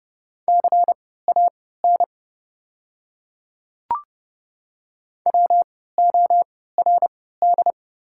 Morse code, heard as a keyed tone, about 29 wpm, spaced as for 17 wpm, 690 Hz. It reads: CAN WORD
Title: 3x Character Spacing - Top 100 Words - 30wpm